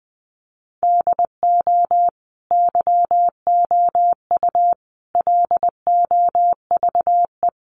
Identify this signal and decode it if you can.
Morse code, heard as a keyed tone, about 20 wpm, 700 Hz. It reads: DO YOU LOVE
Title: Sentences from Top 400 Words 20wpm